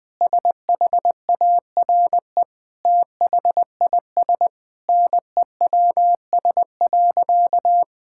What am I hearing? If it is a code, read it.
Morse code, heard as a keyed tone, about 20 wpm, 700 Hz. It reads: SHARE THIS NEWS.
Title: Sentences with 8-Word Phrases - 20wpm